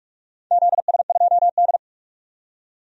Morse code, heard as a keyed tone, about 42 wpm, 700 Hz. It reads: ZS2D
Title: DXCC-100 Call Signs - 4 Characters - 45wpm